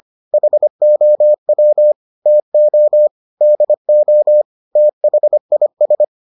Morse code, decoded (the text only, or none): HOW TO DO THIS